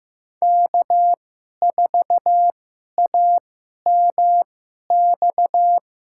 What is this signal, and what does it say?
Morse code, heard as a keyed tone, about 16 wpm, spaced as for 11 wpm, 700 Hz. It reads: K4AMX
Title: USA Call Signs - 5 Characters - 2x Character Spacing - 15wpm